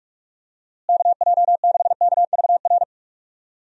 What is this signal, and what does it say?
Morse code, heard as a keyed tone, about 46 wpm, 700 Hz. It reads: KJ6KVR
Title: USA Call Signs - 6 Characters - Repeated 2x - 45wpm